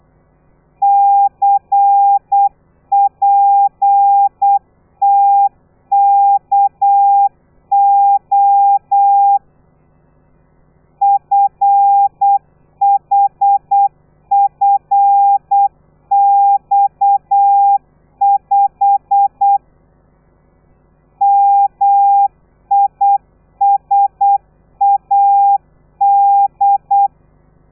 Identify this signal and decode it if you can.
Morse code, heard as a keyed tone, about 8 wpm, 800 Hz. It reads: CPTKO FHFX5 MISAD
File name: telegrafia / i2tfj / Lez13 35